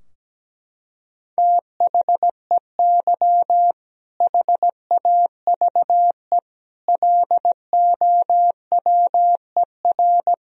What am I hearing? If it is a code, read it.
Morse code, heard as a keyed tone, about 17 wpm, 700 Hz. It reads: THEY HAVE LOWER
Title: Sentences from Top 1000 Words - Review Part 4 - 17wpm